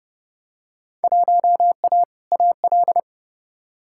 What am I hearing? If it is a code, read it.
Morse code, heard as a keyed tone, about 30 wpm, 700 Hz. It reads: 1A AL
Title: Field Day Exchanges - 30wpm